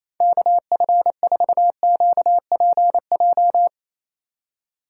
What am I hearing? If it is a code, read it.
Morse code, heard as a keyed tone, about 27 wpm, 700 Hz. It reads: KF4QPJ
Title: USA Call Signs - 6 Characters - 8x Character Spacing - 28wpm